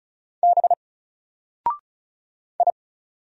Morse code, heard as a keyed tone, about 36 wpm, 700 Hz. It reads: B I
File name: Single-Letter-Number-35wpm